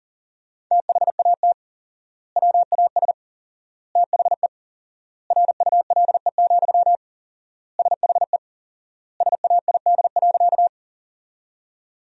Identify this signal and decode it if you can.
Morse code, heard as a keyed tone, about 40 wpm, 700 Hz. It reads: THAT WAS THE RULE, SHE SAID.
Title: 4x Word-Spacing - Sentences from Top 400 Words - 40wpm